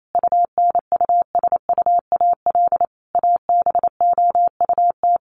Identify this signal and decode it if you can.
Morse code, heard as a keyed tone, about 28 wpm, 700 Hz. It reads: UNUSUAL ABOUT
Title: Sentences from Top 1400 Words 28wpm